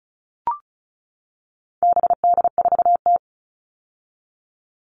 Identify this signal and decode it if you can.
Morse code, heard as a keyed tone, about 34 wpm, 700 Hz. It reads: BD4T
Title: All Cut Numbers - 4-Digit 35wpm